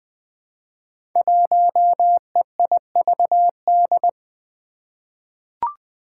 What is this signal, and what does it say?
Morse code, heard as a keyed tone, about 20 wpm, 700 Hz. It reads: 1EIVD
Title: Five Challenging Letter-Number Combinations 20wpm